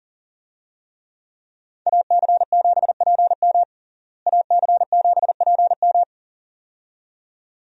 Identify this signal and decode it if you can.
Morse code, heard as a keyed tone, about 40 wpm, 700 Hz. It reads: AC7PM AC7PM
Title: USA Call Signs - 5 Characters - Repeated 2x - 40wpm